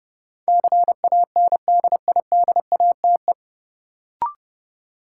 Morse code, heard as a keyed tone, about 30 wpm, 700 Hz. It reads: CANDIDATE
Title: Sentences from Top 1600 Words - New Word Drill - 30wpm